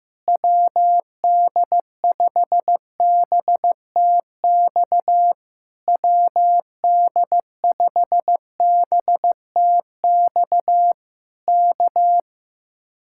Morse code, heard as a keyed tone, about 15 wpm, 700 Hz. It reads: WD5BTX WD5BTX K
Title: Speed-Racing Common QSO Phrases Encore - 15wpm